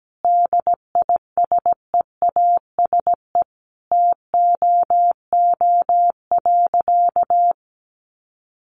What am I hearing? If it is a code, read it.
Morse code, heard as a keyed tone, about 17 wpm, 700 Hz. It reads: DISEASE TOO.
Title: Sentences from Top 1500 Words 17wpm